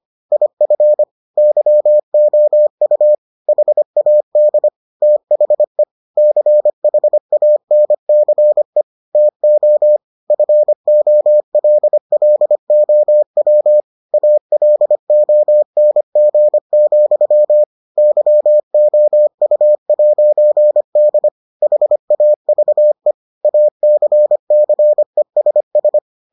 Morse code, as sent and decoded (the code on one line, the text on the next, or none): .. ..-. / -.-- --- ..- / .... .- -.. / - .... / -.-. .... .- -. -.-. / - --- / ..-. --- .-.. .-.. --- .-- / .- .-.. --- -. --. --..-- / -.-- --- ..- .----. -.. / .... .- ...- . / .- -.-. -.-. ... ...
IF YOU HAD THE CHANCE TO FOLLOW ALONG, YOU'D HAVE ACCESS